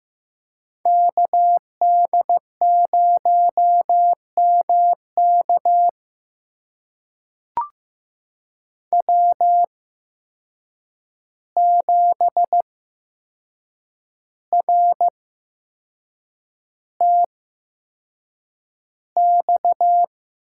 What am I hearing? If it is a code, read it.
Morse code, heard as a keyed tone, about 15 wpm, spaced as for 6 wpm, 700 Hz. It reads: KD0MK W7RTX
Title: USA Call Signs - 5 Characters - 8x Character Spacing - 15wpm